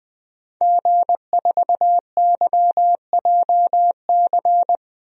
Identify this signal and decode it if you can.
Morse code, heard as a keyed tone, about 20 wpm, 700 Hz. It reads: G4YJC
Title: DXCC-100 Call Signs - 5 Characters - 20wpm